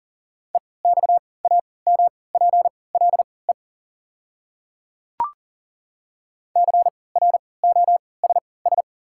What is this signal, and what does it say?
Morse code, heard as a keyed tone, about 38 wpm, spaced as for 22 wpm, 700 Hz. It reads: EXAMPLE CROSS
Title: 3x Character Spacing - Top 500 Words - 40wpm